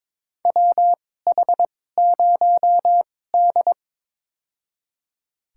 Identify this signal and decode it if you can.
Morse code, heard as a keyed tone, about 22 wpm, spaced as for 16 wpm, 700 Hz. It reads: WH0D
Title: USA Call Signs - 4 Characters - 2x Character Spacing - 22wpm